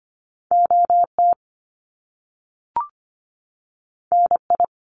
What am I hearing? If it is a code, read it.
Morse code, heard as a keyed tone, about 25 wpm, 700 Hz. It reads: OT NI
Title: Top Two Letter Combinations in Words 25wpm